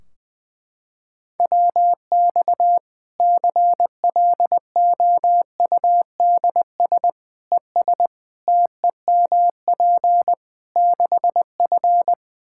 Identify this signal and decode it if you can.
Morse code, heard as a keyed tone, about 20 wpm, 700 Hz. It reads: WX CLOUDS ES TEMP 6F
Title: Practice QSO - Exchange 2 - Info - 20wpm